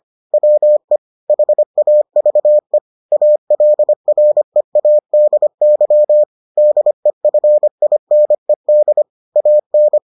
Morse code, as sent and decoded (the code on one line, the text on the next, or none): .-- . / .... .- ...- . / .- .-.. .-. .- -.. -.-- / -.. ..-. .. -. -.. / .- -.
WE HAVE ALREADY DEFINED AN